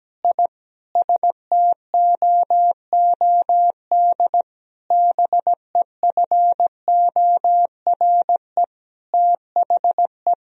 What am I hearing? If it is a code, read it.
Morse code, heard as a keyed tone, about 17 wpm, 700 Hz. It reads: I STOOD BEFORE THE
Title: Sentences from Top 1000 Words - Review Part 2 - 17wpm